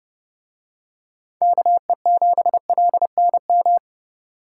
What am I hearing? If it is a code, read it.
Morse code, heard as a keyed tone, about 28 wpm, 700 Hz. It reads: KE7LNM